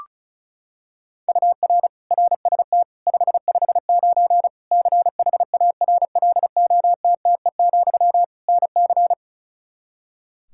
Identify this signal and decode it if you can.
Morse code, heard as a keyed tone, about 36 wpm, 700 Hz. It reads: UR RST 559 CHARLOTTE, NC
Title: Practice QSO - Exchange 1 - Beginning - 35wpm